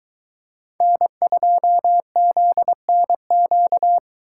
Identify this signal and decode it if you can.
Morse code, heard as a keyed tone, about 23 wpm, 700 Hz. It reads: N2ZNQ